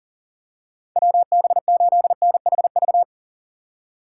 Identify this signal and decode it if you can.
Morse code, heard as a keyed tone, about 40 wpm, 700 Hz. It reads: WB8NHV
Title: USA Call Signs - 6 Characters - 2x Character Spacing - 40wpm